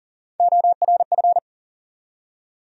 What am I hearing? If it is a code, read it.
Morse code, heard as a keyed tone, about 40 wpm, 700 Hz. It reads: ORF
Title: USA Call Sign Postfixes - 3 Letters 40wpm